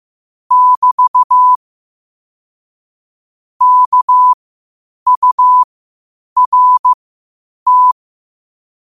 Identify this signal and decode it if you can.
Morse code, heard as a keyed tone, about 15 wpm, spaced as for 8 wpm, 1000 Hz. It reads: <BT> KURT
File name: qotdfiles / 8wpm